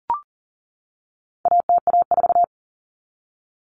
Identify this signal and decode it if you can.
Morse code, heard as a keyed tone, about 40 wpm, 700 Hz. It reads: ATA4
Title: Common Cut Numbers - 4-Digit 40wpm